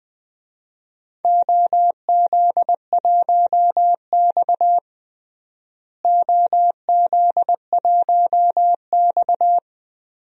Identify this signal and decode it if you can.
Morse code, heard as a keyed tone, about 20 wpm, 700 Hz. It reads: OZ1X OZ1X
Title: DXCC-100 Call Signs - 4 Characters - Repeated 2x - 20wpm